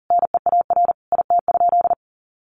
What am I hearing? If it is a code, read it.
Morse code, heard as a keyed tone, about 40 wpm, 700 Hz. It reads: NEAR IT?
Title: Sentences from Top 1400 Words 40wpm